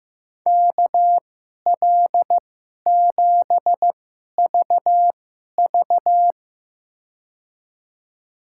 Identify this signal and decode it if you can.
Morse code, heard as a keyed tone, about 15 wpm, spaced as for 11 wpm, 700 Hz. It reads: KL7VV
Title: USA Call Signs - 5 Characters - 2x Character Spacing - 15wpm